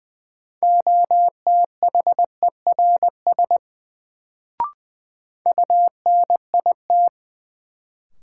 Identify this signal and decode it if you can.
Morse code, heard as a keyed tone, about 20 wpm, 700 Hz. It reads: OTHERS UNIT
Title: Sentences from Top 1000 Words - Review Part 2 - New Word Drill - 20wpm